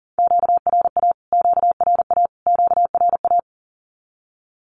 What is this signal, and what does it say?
Morse code, heard as a keyed tone, about 40 wpm, 700 Hz. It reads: QRA QRA QRA
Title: Q-Codes Repeated 3x 40wpm